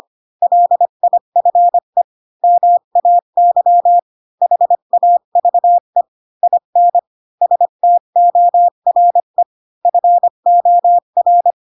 Morse code, heard as a keyed tone, about 25 wpm, 700 Hz. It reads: LIFE MAY HAVE IN STORE FOR
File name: Pinocchio / Pinocchio ch7 20 25 ik1hge